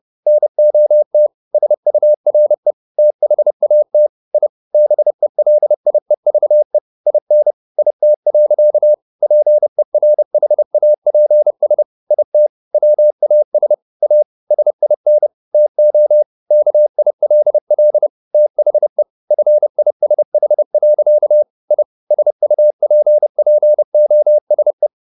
Morse code, heard as a keyed tone, about 30 wpm, 600 Hz. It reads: NOT SURE THAT I BELIEVE IN IT. PERHAPS IT WAS A SIN TO KILL THE FISH. I SUPPOSE